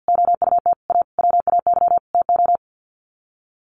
Morse code, heard as a keyed tone, about 49 wpm, 700 Hz. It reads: OUT A WAY TO
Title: Sets of 5 Words Encore2 50wpm